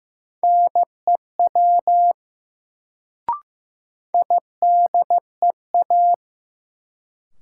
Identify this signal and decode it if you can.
Morse code, heard as a keyed tone, about 16 wpm, 700 Hz. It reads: NEW IDEA
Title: Sentences from Top 300 Words - New Word Drill - 15wpm